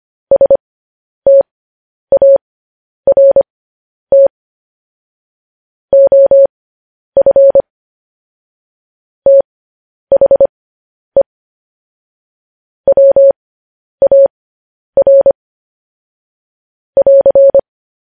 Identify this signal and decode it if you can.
Morse code, heard as a keyed tone, about 25 wpm, spaced as for 10 wpm, 550 Hz. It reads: START OF THE WAR <AR>
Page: news-headlines-in-morse-code-at-10-wpm